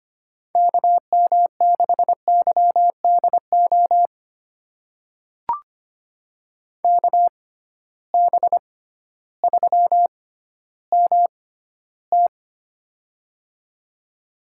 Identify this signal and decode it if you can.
Morse code, heard as a keyed tone, about 25 wpm, spaced as for 13 wpm, 700 Hz. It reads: KM6YDO KB3MT
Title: USA Call Signs - 6x Character Spacing - 25wpm